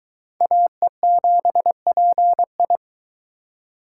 Morse code, heard as a keyed tone, about 24 wpm, 700 Hz. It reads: AE7PI